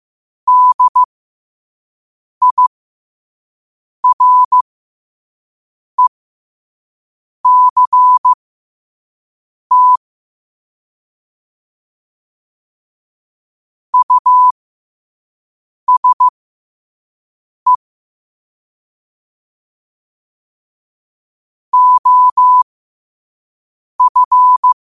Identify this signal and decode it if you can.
Morse code, heard as a keyed tone, about 15 wpm, spaced as for 5 wpm, 1000 Hz. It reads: DIRECT USE OF
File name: 5wpm